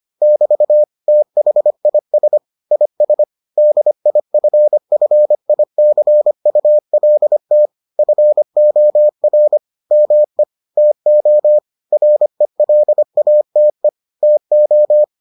Morse code, heard as a keyed tone, about 25 wpm, 600 Hz. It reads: <BT> THIS IS DIFFICULT FOR ME TO RELATE TO